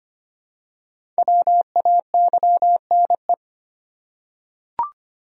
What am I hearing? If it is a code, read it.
Morse code, heard as a keyed tone, about 25 wpm, 700 Hz. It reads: WAYNE
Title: USA Common Names - Max 5 Letters - 25wpm